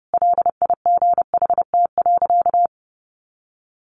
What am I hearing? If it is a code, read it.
Morse code, heard as a keyed tone, about 30 wpm, 700 Hz. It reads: LIGHT.
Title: Sentences from Top 1100 Words 30wpm